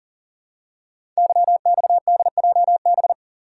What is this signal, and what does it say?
Morse code, heard as a keyed tone, about 42 wpm, 690 Hz. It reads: YXDJB
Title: Code Groups - 5 Letters - 40wpm